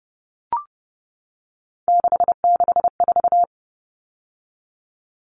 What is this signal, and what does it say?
Morse code, heard as a keyed tone, about 30 wpm, 700 Hz. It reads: 664
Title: Common Cut Numbers - 3-Digit 30wpm